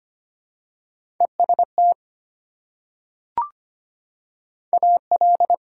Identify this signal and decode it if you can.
Morse code, heard as a keyed tone, about 25 wpm, 700 Hz. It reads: EST AL
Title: Prefixes and Suffixes 25wpm